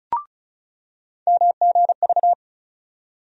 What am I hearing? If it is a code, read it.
Morse code, heard as a keyed tone, about 36 wpm, 700 Hz. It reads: MGV
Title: USA Call Sign Postfixes - 3 Letters 35wpm